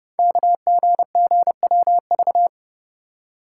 Morse code, heard as a keyed tone, about 28 wpm, 700 Hz. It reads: KGGWV
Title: Code Groups - 5 Letters - 30wpm